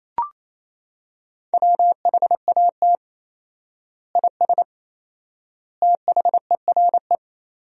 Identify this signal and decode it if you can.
Morse code, heard as a keyed tone, about 28 wpm, 700 Hz. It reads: WHAT IS THERE
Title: 4x Word-Spacing - Sentences from Top 400 Words - 28wpm